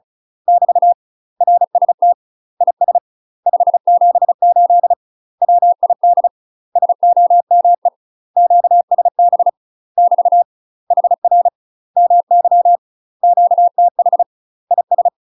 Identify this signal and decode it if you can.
Morse code, heard as a keyed tone, about 34 wpm, 700 Hz. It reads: <BT> RST IS 578 WID SOME QSB <BT> HR MY QTH IS
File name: UK MASTER QSOs TRAINING / 35WPM / UK MASTER QSOs 0043